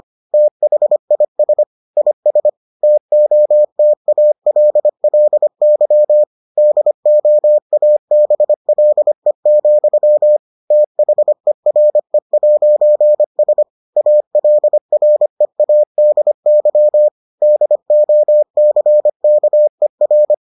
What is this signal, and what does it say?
Morse code, heard as a keyed tone, about 25 wpm, 600 Hz. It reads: THIS IS TOTALLY DOABLE, THERE'S ALREADY DOCKER